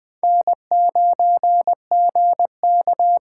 Morse code, heard as a keyed tone, about 20 wpm, 700 Hz. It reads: N9GK
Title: USA Call Signs - 5 Characters - 6x Character Spacing - 20wpm